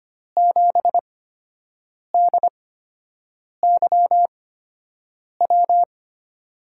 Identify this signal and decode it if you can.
Morse code, heard as a keyed tone, about 25 wpm, spaced as for 7 wpm, 700 Hz. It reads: 7DYW
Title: USA Call Signs - 6 Characters - 8x Character Spacing - 25wpm